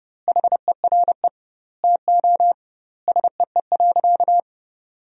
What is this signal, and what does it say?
Morse code, heard as a keyed tone, about 32 wpm, 700 Hz. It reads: HERE TO SEE.
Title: 2x Word-Spacing - Sentences from Top 200 Words - 30wpm